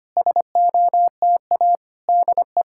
Morse code, heard as a keyed tone, about 25 wpm, 700 Hz. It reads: SOTA DE
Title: SOTA QSO Practice 25wpm